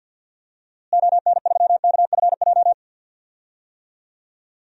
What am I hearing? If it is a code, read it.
Morse code, heard as a keyed tone, about 46 wpm, 700 Hz. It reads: ON3KFJ
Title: DXCC-100 Call Signs - 6 Characters - Repeated 3x - 50wpm